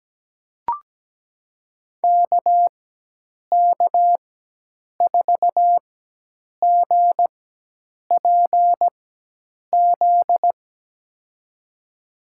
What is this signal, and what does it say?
Morse code, heard as a keyed tone, about 17 wpm, spaced as for 8 wpm, 700 Hz. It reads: KK4GPZ
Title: USA Call Signs - 6 Characters - 4x Character Spacing - 17wpm